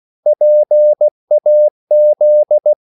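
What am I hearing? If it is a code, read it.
Morse code, heard as a keyed tone, about 16 wpm, 600 Hz. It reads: PAZ
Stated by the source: CW practice of national capitals at 16 words per minute, from ON6ZQ.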